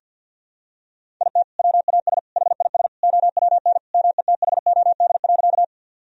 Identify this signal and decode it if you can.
Morse code, heard as a keyed tone, about 51 wpm, 700 Hz. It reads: IT WAS HIS OWN METHOD.
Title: Sentences from Top 1000 Words - Review Part 3 - 50wpm